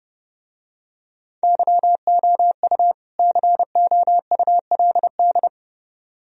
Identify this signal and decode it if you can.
Morse code, heard as a keyed tone, about 29 wpm, 700 Hz. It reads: YOU COULD